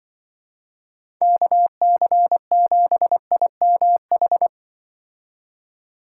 Morse code, heard as a keyed tone, about 24 wpm, 700 Hz. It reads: KC7IMH
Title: Speed Racing - USA Call Signs 20wpm